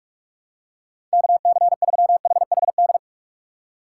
Morse code, heard as a keyed tone, about 41 wpm, 700 Hz. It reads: KC3HHD
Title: USA Call Signs - 6 Characters - 8x Character Spacing - 45wpm